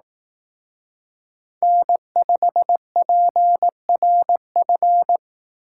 Morse code, heard as a keyed tone, about 18 wpm, 700 Hz. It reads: N5PRF
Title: Speed Racing - USA 1x3 Call Signs 15wpm